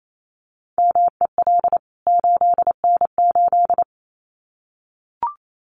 Morse code, heard as a keyed tone, about 28 wpm, 700 Hz. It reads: MEL 8N8
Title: CWT Exchange 28wpm